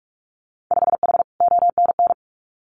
Morse code, heard as a keyed tone, about 46 wpm, 700 Hz. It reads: 5H ONN
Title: Winter Field Day Exchanges Encore 45wpm